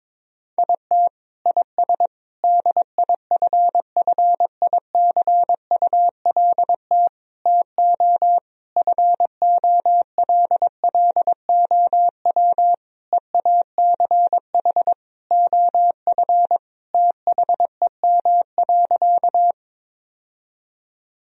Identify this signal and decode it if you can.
Morse code, heard as a keyed tone, about 22 wpm, 700 Hz. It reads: IT IS DIFFICULT TO FOLLOW EACH OF THEM.
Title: Sentences from Top 1000 Words - Review Part 1 - 22wpm